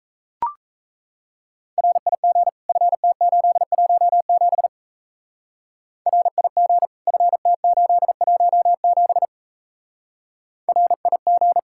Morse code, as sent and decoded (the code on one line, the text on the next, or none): .-. .. --. / ..-. - ---.. .---- --... / .-. .. --. / ..-. - ---.. .---- --... / .-. .. --.
RIG FT817 RIG FT817 RIG